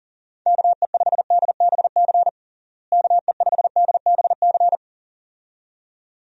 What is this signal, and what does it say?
Morse code, heard as a keyed tone, about 40 wpm, 700 Hz. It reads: KE5DBC KE5DBC
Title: USA Call Signs - 6 Characters - Repeated 2x - 40wpm